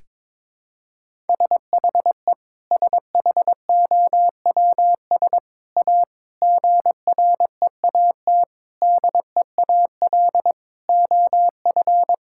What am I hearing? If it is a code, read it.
Morse code, heard as a keyed tone, about 22 wpm, 700 Hz. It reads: SHE SHOWS A GREAT DEAL OF
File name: practice / 22wpm / Sentences with 4-Word Phrases - 22wpm